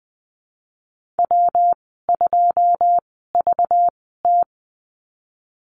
Morse code, heard as a keyed tone, about 20 wpm, spaced as for 14 wpm, 700 Hz. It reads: W2VT
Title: 219 Lesson 35g - Number 0 - Call Signs 20wpm